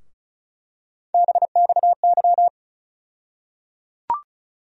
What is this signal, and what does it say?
Morse code, heard as a keyed tone, about 36 wpm, 700 Hz. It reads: BXY
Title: USA Call Sign Postfixes - 3 Letters 35wpm